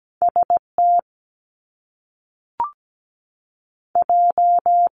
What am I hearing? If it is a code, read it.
Morse code, heard as a keyed tone, about 17 wpm, 700 Hz. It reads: ST J
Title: Top Two Letter Combinations in Words 17wpm